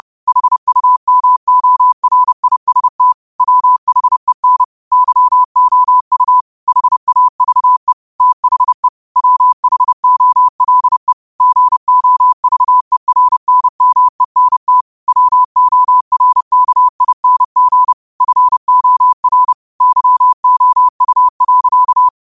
Morse code, heard as a keyed tone, about 30 wpm, 1000 Hz. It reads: HUMORIST WHEN YOU HAVE THE WHOLE GOVERNMENT WORKING FOR YOU.
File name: qotdfiles / 30wpm